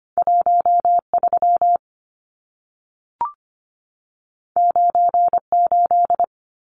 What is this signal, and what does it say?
Morse code, heard as a keyed tone, about 25 wpm, 700 Hz. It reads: 13 98
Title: Numbers - 2-Digit 25wpm